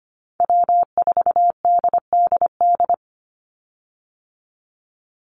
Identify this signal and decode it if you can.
Morse code, heard as a keyed tone, about 25 wpm, 700 Hz. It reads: W4DDD
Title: Call Signs - 25wpm